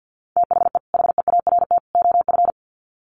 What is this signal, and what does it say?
Morse code, heard as a keyed tone, about 50 wpm, 700 Hz. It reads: THE HEART OF
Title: Sets of 3 Words 50wpm